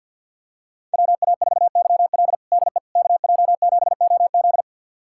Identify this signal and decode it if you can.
Morse code, heard as a keyed tone, about 50 wpm, 700 Hz. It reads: WA4YL DE KJ7OZ
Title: USA Callsign DE Callsign 50wpm